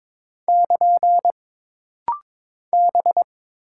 Morse code, heard as a keyed tone, about 22 wpm, 700 Hz. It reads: <KN> B
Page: This rapid-fire timing is geared to push your ICR (Instant Character Recognition) skills.